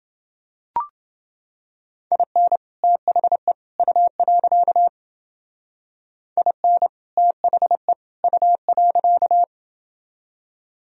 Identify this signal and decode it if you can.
Morse code, heard as a keyed tone, about 29 wpm, 700 Hz. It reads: IN THE U. IN THE U.